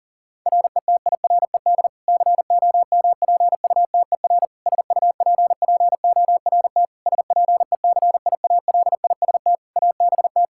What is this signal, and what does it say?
Morse code, heard as a keyed tone, about 40 wpm, 700 Hz. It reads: RETIRED COMPUTER SUPPORT SPECIALIST ABT
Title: Practice QSO - Exchange 3 - More Info - 40wpm